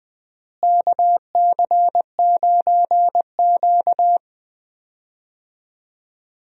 Morse code, heard as a keyed tone, about 20 wpm, 700 Hz. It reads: KC9Q
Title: USA Call Signs - 4 Characters - Repeated 3x - 20wpm